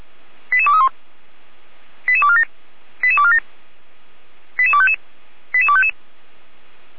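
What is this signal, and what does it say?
Five-tone selective call (ZVEI), heard as quick runs of five tones, tones of about 70 ms. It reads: 88231 88247 88247 88255 88255
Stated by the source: ZVEI five-tone selective call received on SDR.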